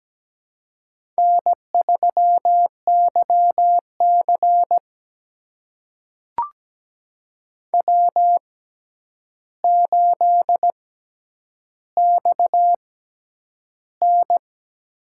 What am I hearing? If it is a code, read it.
Morse code, heard as a keyed tone, about 17 wpm, spaced as for 9 wpm, 700 Hz. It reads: N3YC W8XN
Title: USA Call Signs - 4 Characters - 6x Character Spacing - 17wpm